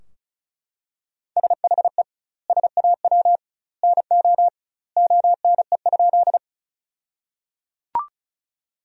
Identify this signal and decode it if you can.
Morse code, heard as a keyed tone, about 36 wpm, 700 Hz. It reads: SHE SAW NO ONE?
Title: 2x Word-Spacing - Sentences from Top 300 Words - 35wpm